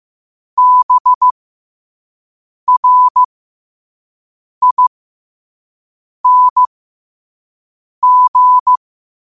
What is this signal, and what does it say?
Morse code, heard as a keyed tone, about 15 wpm, spaced as for 5 wpm, 1000 Hz. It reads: BRING
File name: qotdfiles / 5wpm